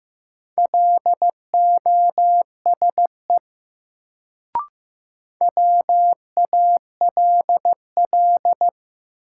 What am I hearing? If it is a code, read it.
Morse code, heard as a keyed tone, about 15 wpm, 700 Hz. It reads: LOSE WALL
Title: Sentences from Top 1100 Words - New Word Drill Alt - 15wpm